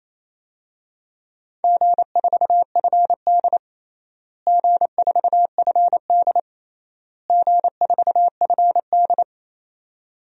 Morse code, heard as a keyed tone, about 28 wpm, 700 Hz. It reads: G4FD G4FD G4FD
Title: DXCC-100 Call Signs - 4 Characters - Repeated 3x - 28wpm